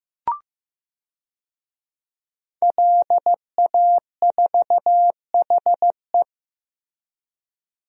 Morse code, heard as a keyed tone, about 16 wpm, 700 Hz. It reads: LA4HE
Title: DXCC-100 Call Signs - 5 Characters - Repeated 3x - 15wpm